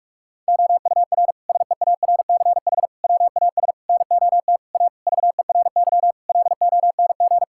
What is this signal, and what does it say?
Morse code, heard as a keyed tone, about 46 wpm, 700 Hz. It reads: OUR SEARCH WAS NOT A VERY LONG
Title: Sentences from Top 1000 Words - Review Part 5 - 45wpm